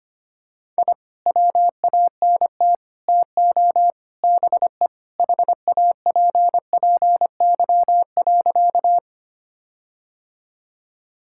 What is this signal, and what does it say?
Morse code, heard as a keyed tone, about 25 wpm, 700 Hz. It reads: I WANT TO BE HAPPY.
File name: practice / 25wpm / Sentences with 4-Word Phrases - 25wpm